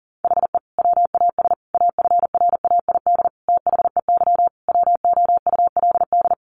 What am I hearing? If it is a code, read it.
Morse code, heard as a keyed tone, about 40 wpm, 700 Hz. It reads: HE WAS AFRAID THEY WOULD
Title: Sentences from Top 1500 Words 40wpm